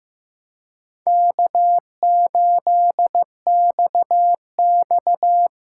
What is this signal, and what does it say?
Morse code, heard as a keyed tone, about 15 wpm, 700 Hz. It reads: K8XX
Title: USA Call Signs - 4 Characters - 6x Character Spacing - 15wpm